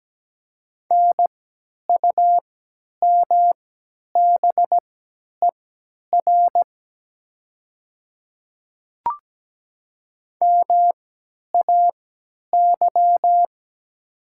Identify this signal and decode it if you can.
Morse code, heard as a keyed tone, about 17 wpm, spaced as for 10 wpm, 700 Hz. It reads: NUMBER MAY
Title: 3x Character Spacing - Top 100 Words - 17wpm